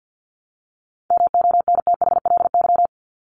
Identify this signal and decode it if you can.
Morse code, heard as a keyed tone, about 48 wpm, 700 Hz. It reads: MONTHLY
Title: LY Suffix - Top 100 Words with Base Words 50wpm